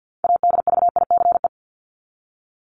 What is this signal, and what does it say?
Morse code, heard as a keyed tone, about 50 wpm, 700 Hz. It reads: ADVICE
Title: Top 1001-1250 Words 50wpm